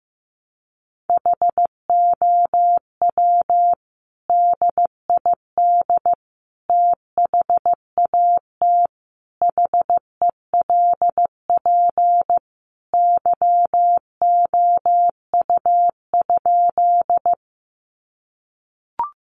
Morse code, heard as a keyed tone, about 15 wpm, 700 Hz. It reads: HOW DID THAT HELP YOU?
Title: Sentences from Top 200 Words 15wpm